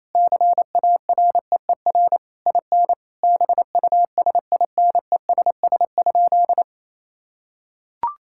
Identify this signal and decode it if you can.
Morse code, heard as a keyed tone, about 28 wpm, 690 Hz. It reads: CAREER IN BUSINESS?
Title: Sentences from Top 1000 Words - Review Part 6 - 28wpm